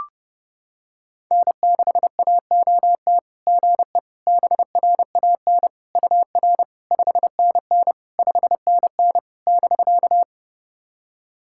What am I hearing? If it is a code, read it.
Morse code, heard as a keyed tone, about 30 wpm, 660 Hz. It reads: N6AOT GE BRAN UR 5NN 5NN <BK>
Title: SOTA QSO Practice 30wpm